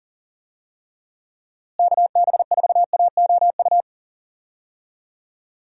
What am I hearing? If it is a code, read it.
Morse code, heard as a keyed tone, about 40 wpm, 700 Hz. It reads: KB4AOU
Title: USA Call Signs - 6 Characters - Repeated 3x - 40wpm